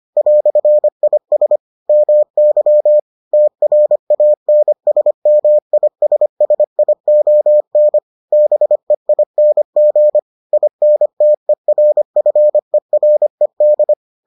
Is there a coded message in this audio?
Yes: Morse code from 0.2 to 13.9 s, heard as a keyed tone, about 25 wpm, 600 Hz. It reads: "IS MY TRANSMISSION BEING INTERFERED